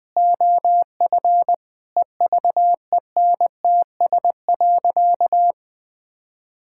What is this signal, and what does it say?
Morse code, heard as a keyed tone, about 21 wpm, 700 Hz. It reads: OF EVENTS.
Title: Sentences with 8-Word Phrases - 20wpm